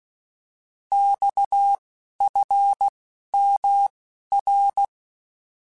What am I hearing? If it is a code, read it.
Morse code, heard as a keyed tone, about 16 wpm, 800 Hz. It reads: X F M R
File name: Abreviations / Abreviations mp3s / Abreviations@10